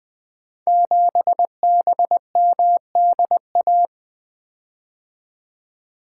Morse code, heard as a keyed tone, about 20 wpm, 700 Hz. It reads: 7BMDA
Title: Code Groups - 5 Letter:Numbers - 20wpm